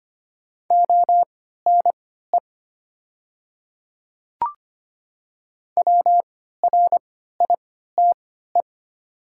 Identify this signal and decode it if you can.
Morse code, heard as a keyed tone, about 25 wpm, spaced as for 14 wpm, 700 Hz. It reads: ONE WRITE
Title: 3x Character Spacing - Top 500 Words - 25wpm